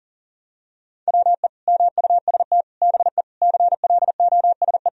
Morse code, heard as a keyed tone, about 37 wpm, 700 Hz. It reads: WE MUST BE CLOSE
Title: Sentences from Top 1000 Words - Review Part 1 - 40wpm